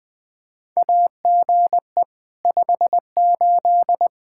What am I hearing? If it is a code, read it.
Morse code, heard as a keyed tone, about 20 wpm, 700 Hz. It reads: AGE 58
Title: Practice QSO - Exchange 3 - More Info - 20wpm